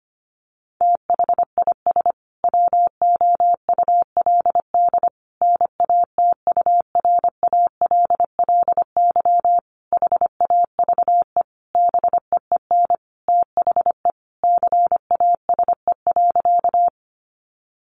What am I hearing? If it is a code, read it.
Morse code, heard as a keyed tone, about 25 wpm, 700 Hz. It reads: THIS WOULD NATURALLY HAVE BEEN THE CASE.
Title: Sentences from Top 1300 Words 25wpm